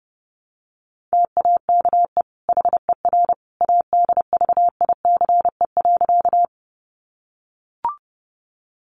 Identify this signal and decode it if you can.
Morse code, heard as a keyed tone, about 30 wpm, 700 Hz. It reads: TAKE HER ADVICE.
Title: Sentences from Top 1100 Words 30wpm